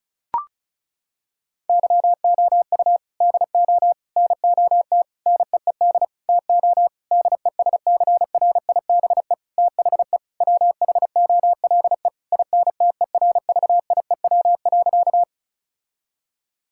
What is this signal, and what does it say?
Morse code, heard as a keyed tone, about 36 wpm, 700 Hz. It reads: YOU DO NOT NEED TO DESCRIBE THE WHOLE INTERVIEW.
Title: Sentences from Top 1600 Words 35wpm